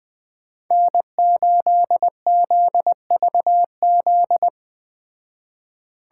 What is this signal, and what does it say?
Morse code, heard as a keyed tone, about 20 wpm, 700 Hz. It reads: N8ZVZ